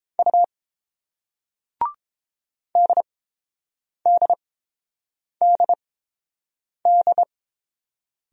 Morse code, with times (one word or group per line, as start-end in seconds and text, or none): 0.2-0.4 U
2.8-7.2 DDDD